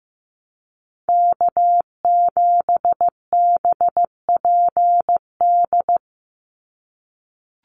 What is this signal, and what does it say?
Morse code, heard as a keyed tone, about 15 wpm, 700 Hz. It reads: K7BPD
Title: USA Call Signs 15wpm